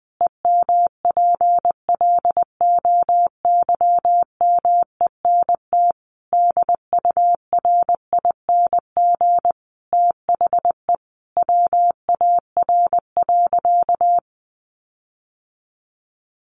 Morse code, heard as a keyed tone, about 20 wpm, 700 Hz. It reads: EMPLOYMENT DURING THE WAR.
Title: Sentences from Top 1500 Words 20wpm